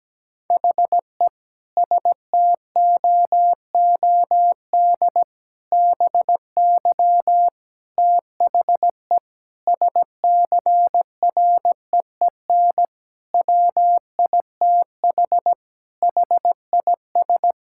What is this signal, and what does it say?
Morse code, heard as a keyed tone, about 17 wpm, 700 Hz. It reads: HE STOOD BY THE SCREEN WITH HIS
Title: Sentences from Top 1000 Words - Review Part 6 - 17wpm